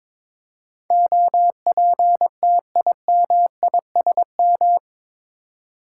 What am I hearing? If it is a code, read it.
Morse code, heard as a keyed tone, about 22 wpm, 700 Hz. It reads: OPTIMISM
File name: practice / 22wpm / Eight Letter Words - 22wpm